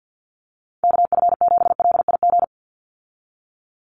Morse code, heard as a keyed tone, about 50 wpm, 700 Hz. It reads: KF7LIG
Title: USA Call Signs 50wpm